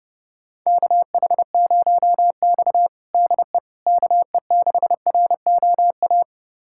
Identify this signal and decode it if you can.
Morse code, heard as a keyed tone, about 30 wpm, 700 Hz. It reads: KH0X DE KE6ROA
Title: USA Callsign DE Callsign 30wpm